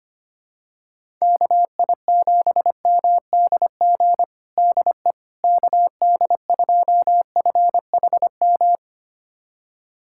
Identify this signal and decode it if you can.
Morse code, heard as a keyed tone, about 25 wpm, 700 Hz. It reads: KI7MDG DE KD2FHM
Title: USA Callsign DE Callsign 25wpm